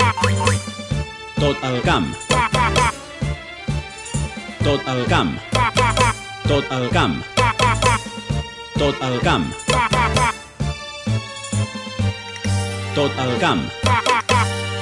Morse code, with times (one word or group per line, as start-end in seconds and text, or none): none